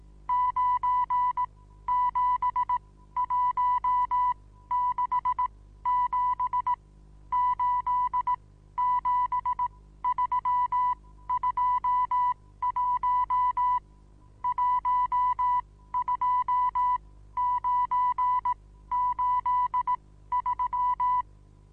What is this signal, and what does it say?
Morse code, heard as a keyed tone, about 18 wpm, spaced as for 13 wpm, 990 Hz. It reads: 97167 87321 12983
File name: Onlinelehrgang / morsen / DJ4UF Morsen25